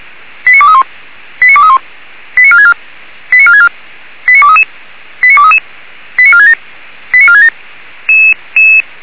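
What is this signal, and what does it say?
Five-tone selective call (ZVEI), heard as quick runs of five tones, tones of about 70 ms. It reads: 89231 89231 89464 89464 89233 89233 89467 89467
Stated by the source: Five-tone ZVEI selective call received on an SDR.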